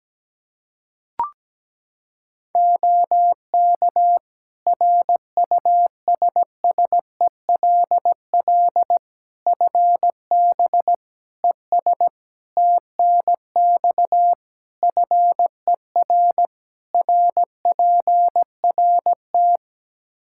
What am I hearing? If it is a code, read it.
Morse code, heard as a keyed tone, about 17 wpm, 700 Hz. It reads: OK RUSSELL FB ES TNX FER RPRT
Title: Practice QSO - Exchange 2 - Info - 17wpm